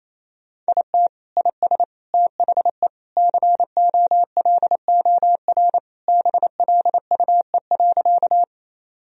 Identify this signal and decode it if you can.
Morse code, heard as a keyed tone, about 28 wpm, 700 Hz. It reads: IT IS THE COLOR BLUE.
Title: Sentences from Top 1000 Words - Review Part 3 - 28wpm